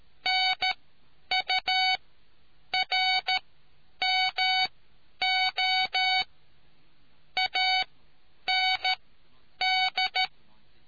Morse code, heard as a keyed tone, about 13 wpm, spaced as for 9 wpm, 750 Hz. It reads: NURMO AND